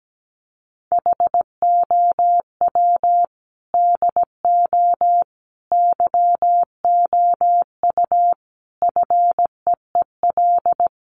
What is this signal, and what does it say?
Morse code, heard as a keyed tone, about 17 wpm, 700 Hz. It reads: HOW DO YOU FEEL
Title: Sets of 5 Words 17wpm